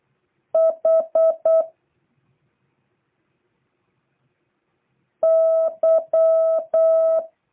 Morse code, heard as a keyed tone, about 8 wpm, 650 Hz. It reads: H Y